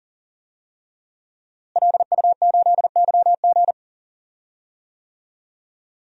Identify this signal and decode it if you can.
Morse code, heard as a keyed tone, about 40 wpm, 700 Hz. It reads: LU8YG
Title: DXCC-100 Call Signs - 5 Characters - Repeated 3x - 40wpm